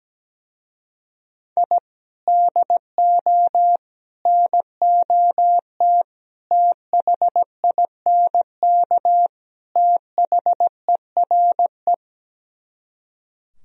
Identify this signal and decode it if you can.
Morse code, heard as a keyed tone, about 17 wpm, 700 Hz. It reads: I DO NOT THINK THERE
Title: Speed-Racing - Sets of 5 Words 17wpm